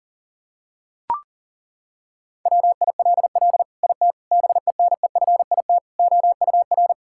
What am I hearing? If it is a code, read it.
Morse code, heard as a keyed tone, about 37 wpm, 700 Hz. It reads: WILL IT BENEFIT OUR
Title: Sentences from Top 1000 Words - Review Part 6 - 40wpm